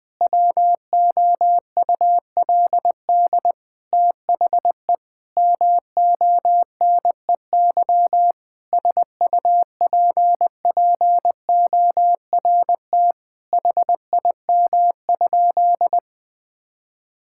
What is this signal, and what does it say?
Morse code, heard as a keyed tone, about 20 wpm, 700 Hz. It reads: WOULD THE MONEY SUPPORT HIM?
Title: Sentences from Top 1000 Words - Review Part 5 - 20wpm